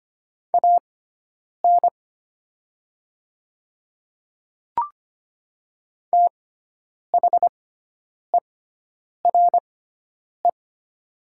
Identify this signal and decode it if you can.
Morse code, heard as a keyed tone, about 25 wpm, spaced as for 9 wpm, 690 Hz. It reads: AN THERE
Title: 6x Character Spacing - Top 200 Words - 25wpm